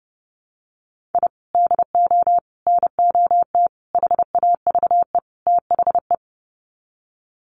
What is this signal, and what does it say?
Morse code, heard as a keyed tone, about 30 wpm, 700 Hz. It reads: I DO NOT HAVE THE
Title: Sets of 5 Words 30wpm